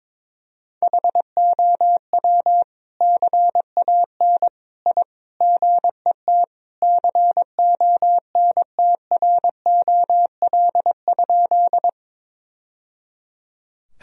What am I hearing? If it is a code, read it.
Morse code, heard as a keyed tone, about 22 wpm, 700 Hz. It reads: HOW CAN I GET CONTROL?
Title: Sentences from Top 1000 Words - Review Part 3 - 22wpm